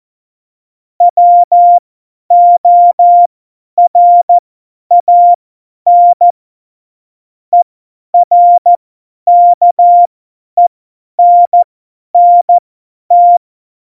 Morse code, heard as a keyed tone, about 14 wpm, spaced as for 10 wpm, 700 Hz. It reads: WORAN ERKENNT
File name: Morsekurs / Witze / Witz 02. Oktober 2018, 50 F70